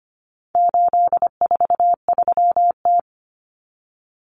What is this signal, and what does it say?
Morse code, heard as a keyed tone, about 25 wpm, 700 Hz. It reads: 843T
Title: Common Cut Numbers - 4-Digit 25wpm